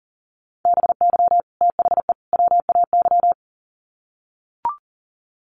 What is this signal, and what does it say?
Morse code, heard as a keyed tone, about 40 wpm, 700 Hz. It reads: BY THE WAY
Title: Sets of 3 Words 40wpm